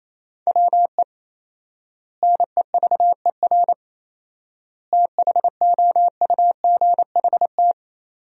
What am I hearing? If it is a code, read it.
Morse code, heard as a keyed tone, about 28 wpm, 700 Hz. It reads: WE NEVER THOUGHT
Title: 4x Word-Spacing - Sentences from Top 300 Words - 28wpm